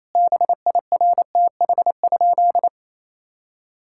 Morse code, heard as a keyed tone, about 28 wpm, 700 Hz. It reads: BIRTH?
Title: Sentences from Top 1000 Words - Review Part 5 - 28wpm